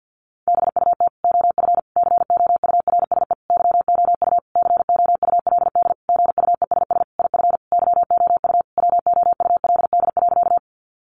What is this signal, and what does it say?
Morse code, heard as a keyed tone, about 50 wpm, 700 Hz. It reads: BUT OF COURSE YOU COULD GUESS IF YOU WOULD.
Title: Sentences from Top 1500 Words 50wpm